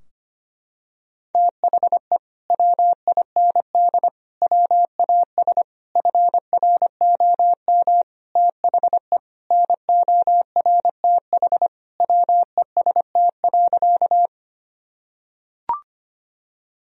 Morse code, heard as a keyed tone, about 25 wpm, 700 Hz. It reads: THE WIND WAS FROM THE NORTH WEST.
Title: Sentences from Top 1000 Words - Review Part 2 - 25wpm